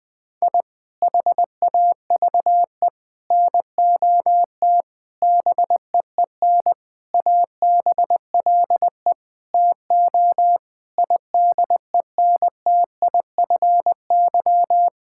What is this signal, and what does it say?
Morse code, heard as a keyed tone, about 21 wpm, 700 Hz. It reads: I HAVE NOT BEEN ABLE TO IDENTIFY
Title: Sentences from Top 1000 Words - Review Part 6 - 20wpm